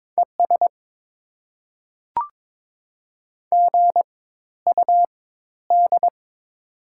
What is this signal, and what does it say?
Morse code, heard as a keyed tone, about 22 wpm, spaced as for 13 wpm, 690 Hz. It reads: ES GUD
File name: Top 20 QSO Elements - 4x Character Spacing - 22wpm